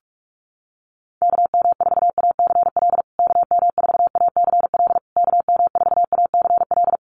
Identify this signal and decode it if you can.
Morse code, heard as a keyed tone, about 44 wpm, 700 Hz. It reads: KM4ACL KM4ACL KM4ACL
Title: USA Call Signs Repeated 3x 45wpm